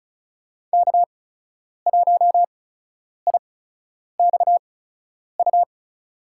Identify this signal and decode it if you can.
Morse code, heard as a keyed tone, about 34 wpm, spaced as for 10 wpm, 700 Hz. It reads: K1IXU
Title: USA Call Signs - 5 Characters - 8x Character Spacing - 35wpm